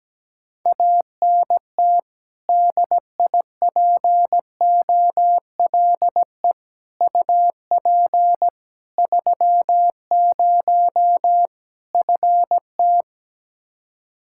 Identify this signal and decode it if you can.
Morse code, heard as a keyed tone, about 17 wpm, 700 Hz. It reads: ANT DIPOLE UP 30 FT
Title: Practice QSO - Exchange 2 - Info - 17wpm